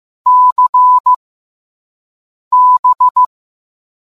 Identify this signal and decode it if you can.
Morse code, heard as a keyed tone, about 15 wpm, spaced as for 5 wpm, 1000 Hz. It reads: CB